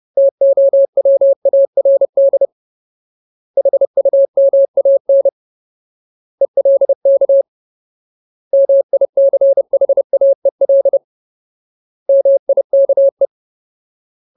Morse code, heard as a keyed tone, about 30 wpm, 550 Hz. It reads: TOWARD HUMAN ELK MICHAEL MIKE
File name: Ultimate Word List / 30WPM Ultimate